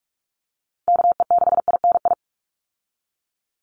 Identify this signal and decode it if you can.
Morse code, heard as a keyed tone, about 44 wpm, 700 Hz. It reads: KE6INI
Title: USA Call Signs 45wpm